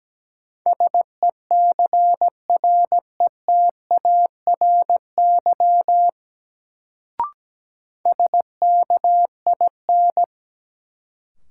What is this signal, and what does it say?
Morse code, heard as a keyed tone, about 17 wpm, 700 Hz. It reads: SECRETARY SKIN
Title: Sentences from Top 1300 Words - New Word Drill - 17wpm